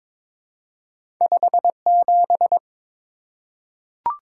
Morse code, heard as a keyed tone, about 22 wpm, 700 Hz. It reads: 57
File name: practice / 22wpm / Two Letter-Number Combinations - 22wpm